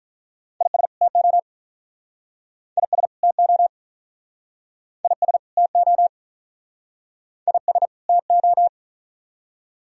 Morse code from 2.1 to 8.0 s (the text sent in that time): IS TO IS TO IS